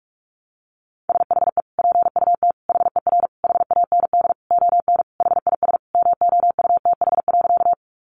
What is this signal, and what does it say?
Morse code, heard as a keyed tone, about 44 wpm, 700 Hz. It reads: SHE PUT HER HAND ON HIS MOUTH.
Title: Sentences from Top 1500 Words 45wpm